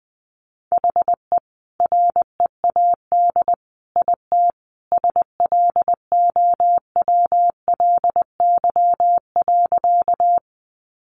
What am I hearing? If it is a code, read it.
Morse code, heard as a keyed tone, about 20 wpm, 700 Hz. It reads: HE READ IT SLOWLY.
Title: Sentences from Top 1100 Words 20wpm